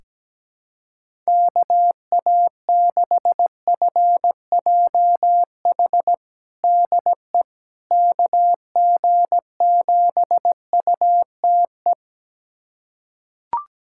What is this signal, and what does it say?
Morse code, heard as a keyed tone, about 17 wpm, 700 Hz. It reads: KA6FJH DE KG7UTE
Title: USA Callsign DE Callsign - 6 Characters - 17wpm